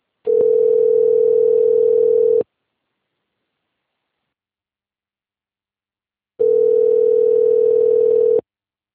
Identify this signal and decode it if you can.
Ringback tone (North America), 440 + 480 Hz.
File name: rg-1600-unknown- - - .6329